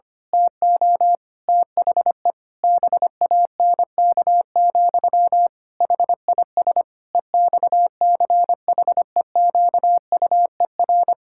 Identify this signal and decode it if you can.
Morse code, heard as a keyed tone, about 25 wpm, 700 Hz. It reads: TO THE BANK, HIS EXCHEQUER
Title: Pangram Challenge v2 - 25wpm